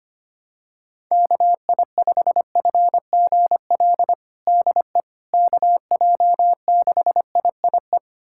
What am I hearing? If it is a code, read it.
Morse code, heard as a keyed tone, about 25 wpm, 700 Hz. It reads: KI5FGL DE KJ6IIE
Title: USA Callsign DE Callsign - 6 Characters - 25wpm